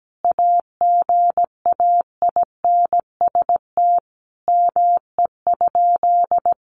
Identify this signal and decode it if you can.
Morse code, heard as a keyed tone, about 17 wpm, 700 Hz. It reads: AGAINST ME?
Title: Sentences from Top 1000 Words 17wpm